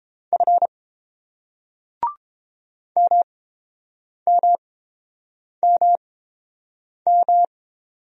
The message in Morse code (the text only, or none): F MMMM